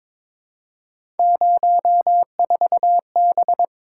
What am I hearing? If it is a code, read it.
Morse code, heard as a keyed tone, about 22 wpm, 700 Hz. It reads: 04B
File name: practice / 22wpm / Three Letter-Number Combinations - 22wpm